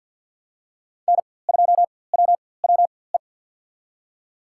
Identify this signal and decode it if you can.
Morse code, heard as a keyed tone, about 51 wpm, spaced as for 23 wpm, 700 Hz. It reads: N2WWE
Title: USA Call Signs - 5 Characters - 4x Character Spacing - 50wpm